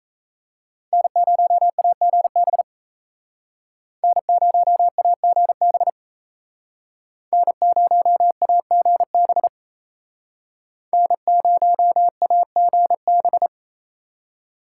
Morse code, heard as a keyed tone, about 37 wpm, 700 Hz. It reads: N0AGB N0AGB N0AGB N0AGB